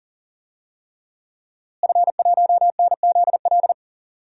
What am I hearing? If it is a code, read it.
Morse code, heard as a keyed tone, about 40 wpm, 700 Hz. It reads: F1NZL
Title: DXCC-100 Call Signs - 5 Characters - Repeated 3x - 40wpm